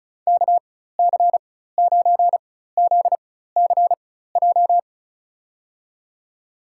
Morse code, heard as a keyed tone, about 34 wpm, spaced as for 16 wpm, 700 Hz. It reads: KC9ZCJ
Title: USA Call Signs - 6 Characters - 4x Character Spacing - 35wpm